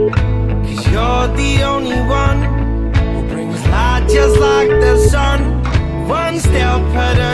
Ringback tone (Europe, 425 Hz), with noise.